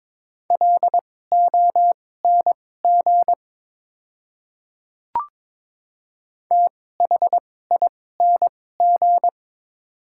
Morse code, heard as a keyed tone, about 22 wpm, spaced as for 16 wpm, 700 Hz. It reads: LONG THING
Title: 2x Character Spacing - Top 200 Words - 22wpm